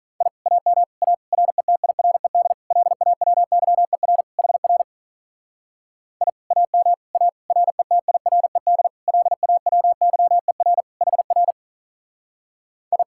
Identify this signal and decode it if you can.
Morse code, heard as a keyed tone, about 44 wpm, 700 Hz. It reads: I AM A RETIRED LAWYER HR I AM A RETIRED LAWYER HR I